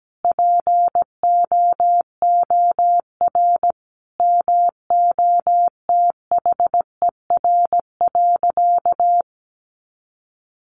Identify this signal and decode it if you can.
Morse code, heard as a keyed tone, about 17 wpm, 700 Hz. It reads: POOR MOTHER.
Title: Sentences from Top 1000 Words 17wpm